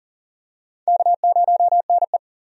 Morse code, heard as a keyed tone, about 40 wpm, 700 Hz. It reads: K0NE
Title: USA Call Signs - 4 Characters - 40wpm